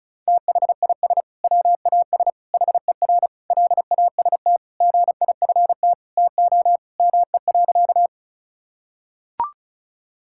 Morse code, heard as a keyed tone, about 34 wpm, 700 Hz. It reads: THIS WAS HER LAST GIFT TO ME.
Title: Sentences from Top 1000 Words - Review Part 6 - 35wpm